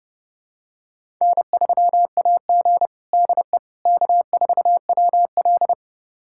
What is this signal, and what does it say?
Morse code, heard as a keyed tone, about 32 wpm, 700 Hz. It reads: N3AG DE K4WL
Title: USA Callsign DE Callsign - 4 Characters - 30wpm